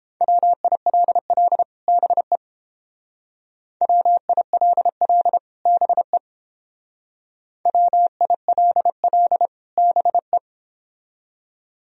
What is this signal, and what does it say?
Morse code, heard as a keyed tone, about 31 wpm, 700 Hz. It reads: WILL BE WILL BE WILL BE